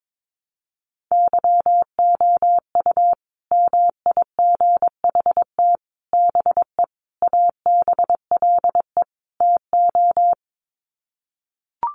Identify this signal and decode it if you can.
Morse code, heard as a keyed tone, about 22 wpm, 700 Hz. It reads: YOU MIGHT BE ABLE TO
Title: Sets of 5 Words Encore2 22wpm